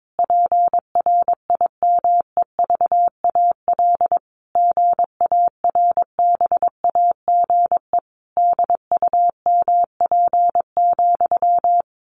Morse code, heard as a keyed tone, about 22 wpm, 700 Hz. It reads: PRIMEVAL GARBAGE DUMP,
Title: A Sound Of Thunder 22wpm